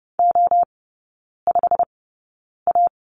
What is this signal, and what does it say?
Morse code, heard as a keyed tone, about 30 wpm, spaced as for 9 wpm, 700 Hz. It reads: O5A
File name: Lesson 16e - Letter c - Three Characters 30wpm